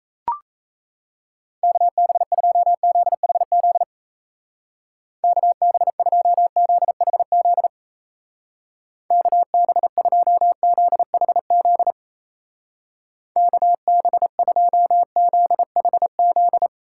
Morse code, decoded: KB2ZHZ KB2ZHZ KB2ZHZ KB2ZHZ

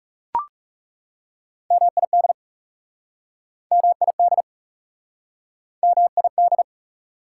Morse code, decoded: MID MID MID

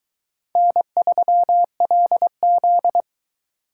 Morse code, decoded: N3LZ

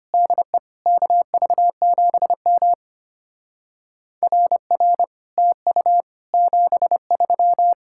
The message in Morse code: DE KV7M RR TU 73

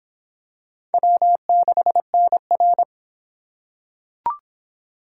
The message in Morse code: W6NR

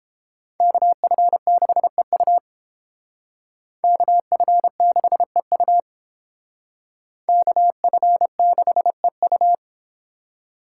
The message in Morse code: KF6EU KF6EU KF6EU

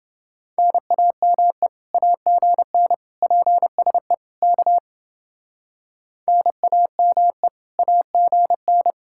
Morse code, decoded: NAME AGN PSE K NAME AGN

